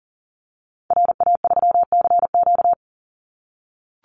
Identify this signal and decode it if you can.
Morse code, heard as a keyed tone, about 40 wpm, 700 Hz. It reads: RA3CQ
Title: Call Signs - 40wpm